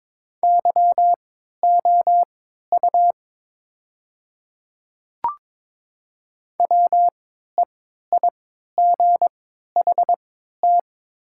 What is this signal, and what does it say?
Morse code, heard as a keyed tone, about 22 wpm, spaced as for 12 wpm, 700 Hz. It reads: YOU WEIGHT
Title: 3x Character Spacing - Top 500 Words - 22wpm